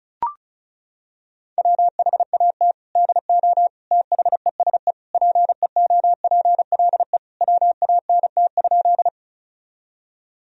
Morse code, heard as a keyed tone, about 36 wpm, 700 Hz. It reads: WHAT DO THESE PEOPLE WANT?
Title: Sentences from Top 1000 Words - Review Part 1 - 35wpm